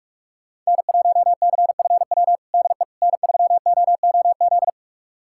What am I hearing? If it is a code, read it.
Morse code, heard as a keyed tone, about 46 wpm, 700 Hz. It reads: N1CFW DE N3OOZ
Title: USA Callsign DE Callsign 45wpm